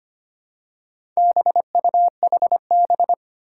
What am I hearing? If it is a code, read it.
Morse code, heard as a keyed tone, about 25 wpm, 700 Hz. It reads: BUHB
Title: Four Challenging Letter-Number Combinations 25wpm